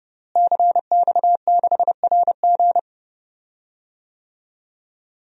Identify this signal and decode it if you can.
Morse code, heard as a keyed tone, about 30 wpm, 700 Hz. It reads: CX6RG
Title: DXCC-100 Call Signs - 5 Characters - Repeated 2x - 30wpm